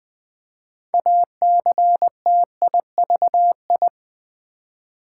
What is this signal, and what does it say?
Morse code, heard as a keyed tone, about 20 wpm, 700 Hz. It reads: ACTIVI
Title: Build A Word - 2 Letters At A Time - 8-Letter Words - 20wpm